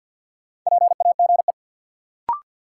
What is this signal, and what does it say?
Morse code, heard as a keyed tone, about 50 wpm, 700 Hz. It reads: PAGE
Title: Sentences from Top 1000 Words - Review Part 1 - New Word Drill - 50wpm